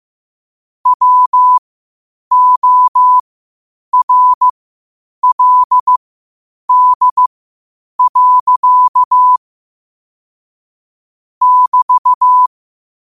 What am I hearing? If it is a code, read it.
Morse code, heard as a keyed tone, about 15 wpm, spaced as for 8 wpm, 1000 Hz. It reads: WORLD. <BT>
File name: qotdfiles / 8wpm